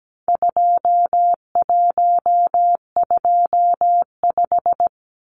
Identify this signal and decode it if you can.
Morse code, heard as a keyed tone, about 17 wpm, 700 Hz. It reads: 2125
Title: Numbers - 4-Digit 17wpm